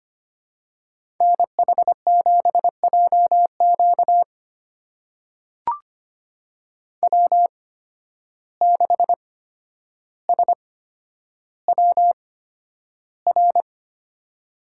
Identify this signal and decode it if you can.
Morse code, heard as a keyed tone, about 25 wpm, spaced as for 11 wpm, 700 Hz. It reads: NH7JQ W6SWR
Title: USA Call Signs - 5 Characters - 8x Character Spacing - 25wpm